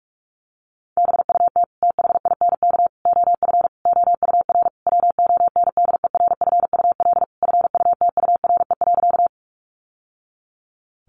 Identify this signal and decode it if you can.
Morse code, heard as a keyed tone, about 44 wpm, 700 Hz. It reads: BUT THINK OF OUR WONDERFUL FUTURE.
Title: Sentences from Top 1100 Words 45wpm